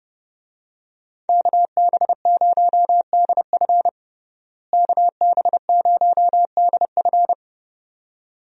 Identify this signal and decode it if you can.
Morse code, heard as a keyed tone, about 32 wpm, 700 Hz. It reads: KB0DF KB0DF
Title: USA Call Signs - 5 Characters - Repeated 2x - 30wpm